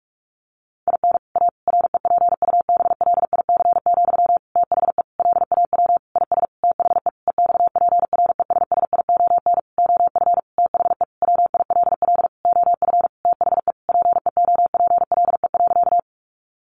Morse code, heard as a keyed tone, about 44 wpm, 700 Hz. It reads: IN A REPUBLIC, THE LAW IS THE EXPRESSION OF THE WILL OF THE PEOPLE.
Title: Sentences from Top 1500 Words 45wpm